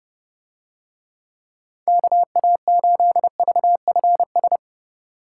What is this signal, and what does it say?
Morse code, heard as a keyed tone, about 28 wpm, 700 Hz. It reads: KA8VFS